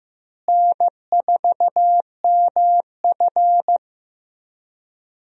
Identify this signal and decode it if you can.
Morse code, heard as a keyed tone, about 15 wpm, 700 Hz. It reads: N4MF
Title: USA Call Signs - 4 Characters - Repeated 3x - 15wpm